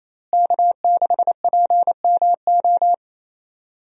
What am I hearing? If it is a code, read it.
Morse code, heard as a keyed tone, about 27 wpm, 700 Hz. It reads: K6PMO
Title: USA Call Signs - 5 Characters - Repeated 2x - 28wpm